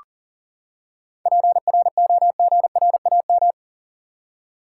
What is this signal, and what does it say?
Morse code, heard as a keyed tone, about 40 wpm, 700 Hz. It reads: PROGRAM
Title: Seven Letter Words - 40wpm